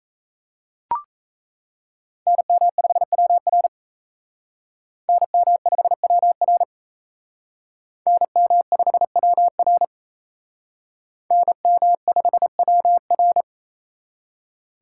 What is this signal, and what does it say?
Morse code, heard as a keyed tone, about 34 wpm, 700 Hz. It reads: NM5WR NM5WR NM5WR NM5WR
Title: Speed Racing - USA Call Signs 28wpm